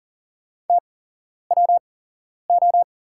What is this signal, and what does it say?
Morse code, heard as a keyed tone, about 40 wpm, spaced as for 11 wpm, 700 Hz. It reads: TWO
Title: 8x Character Spacing - Top 100 Words - 40wpm